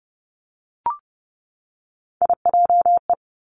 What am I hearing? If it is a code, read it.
Morse code, heard as a keyed tone, about 30 wpm, 700 Hz. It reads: IJE
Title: Three Challenging Letter-Number Combinations 30wpm